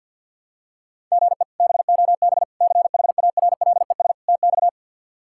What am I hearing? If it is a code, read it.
Morse code, heard as a keyed tone, about 52 wpm, 700 Hz. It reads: GE BOB CHARLES TX
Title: K1USN SST QSO Practice - 50wpm